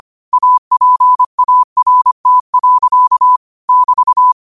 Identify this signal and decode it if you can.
Morse code, heard as a keyed tone, about 25 wpm, 1000 Hz. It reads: APART. <BT>
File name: qotdfiles / 25wpm